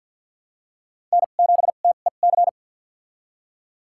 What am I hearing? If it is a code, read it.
Morse code, heard as a keyed tone, about 53 wpm, spaced as for 37 wpm, 700 Hz. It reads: N7TEC
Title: USA Call Signs - 5 Characters - 2x Character Spacing - 50wpm